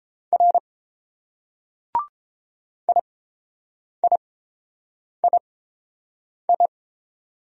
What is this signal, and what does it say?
Morse code, heard as a keyed tone, about 30 wpm, spaced as for 7 wpm, 700 Hz. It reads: R IIII